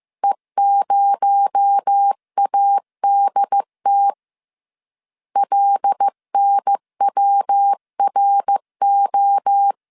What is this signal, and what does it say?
Morse code, heard as a keyed tone, about 15 wpm, 800 Hz. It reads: E0ADT LNWRO